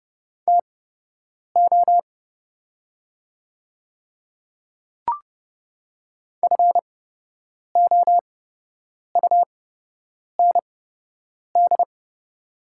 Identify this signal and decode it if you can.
Morse code, heard as a keyed tone, about 30 wpm, spaced as for 8 wpm, 700 Hz. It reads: TO FOUND
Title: 8x Character Spacing - Top 300 Words - 30wpm